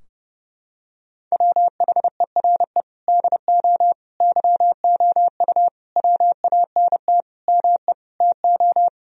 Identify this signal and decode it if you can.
Morse code, heard as a keyed tone, about 30 wpm, 700 Hz. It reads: WHERE DO YOU WANT ME TO
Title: Sentences from Top 1000 Words - Review Part 2 - 30wpm